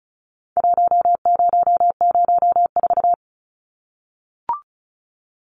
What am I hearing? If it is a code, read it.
Morse code, heard as a keyed tone, about 34 wpm, 700 Hz. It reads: 1004
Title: Numbers - 4-Digit 35wpm